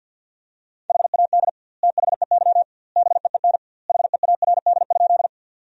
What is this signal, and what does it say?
Morse code, heard as a keyed tone, about 50 wpm, 700 Hz. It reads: HAD THEY BEEN HEARD?
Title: 2x Word-Spacing - Sentences from Top 400 Words - 50wpm